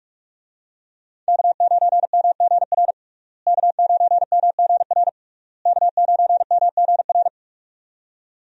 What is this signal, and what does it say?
Morse code, heard as a keyed tone, about 46 wpm, 700 Hz. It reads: K9MGR K9MGR K9MGR
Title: USA Call Signs - 5 Characters - Repeated 3x - 45wpm